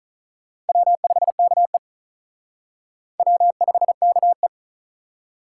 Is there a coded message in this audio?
Yes: Morse code from 0.7 to 4.5 s, heard as a keyed tone, about 45 wpm, 700 Hz. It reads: W5KE W5KE